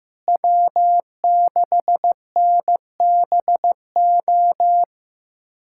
Morse code, heard as a keyed tone, about 15 wpm, 700 Hz. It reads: W6NBO
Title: USA Callsign DE Callsign - 5 Characters - 15wpm